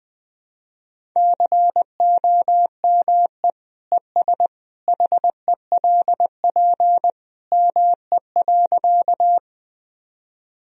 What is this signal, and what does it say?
Morse code, heard as a keyed tone, about 21 wpm, 700 Hz. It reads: COME ES HELP ME.
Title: Sentences from Top 200 Words and Abbreviations 20wpm